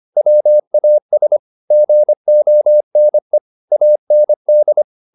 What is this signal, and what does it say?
Morse code, heard as a keyed tone, about 25 wpm, 600 Hz. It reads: WAS GONE AND
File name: the old man and the sea part 1 25wpm